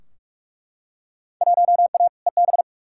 Morse code, heard as a keyed tone, about 43 wpm, 700 Hz. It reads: 1A EB
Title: Field Day Exchanges - 45wpm